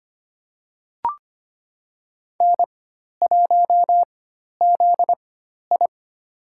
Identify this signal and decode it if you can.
Morse code, heard as a keyed tone, about 25 wpm, spaced as for 12 wpm, 700 Hz. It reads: N1ZI
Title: USA Call Signs - 4x Character Spacing - 25wpm